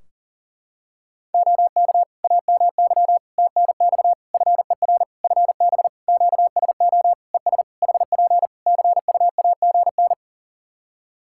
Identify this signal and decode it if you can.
Morse code, heard as a keyed tone, about 40 wpm, 700 Hz. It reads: OK AMY TNX FER FB QSO ES HP CUAGN
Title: Practice QSO - Exchange 4 - Ending - 40wpm